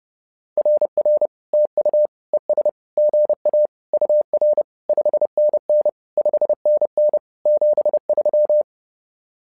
Morse code, heard as a keyed tone, about 30 wpm, 600 Hz. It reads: RR TU ES GA UR 5NN 5NN 73
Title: SOTA QSO Practice 30wpm